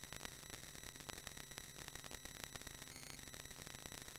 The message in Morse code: none